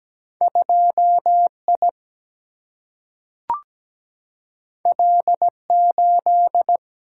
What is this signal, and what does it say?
Morse code, heard as a keyed tone, about 17 wpm, 700 Hz. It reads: 2 I L 8
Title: Single-Letter Single-Number 17wpm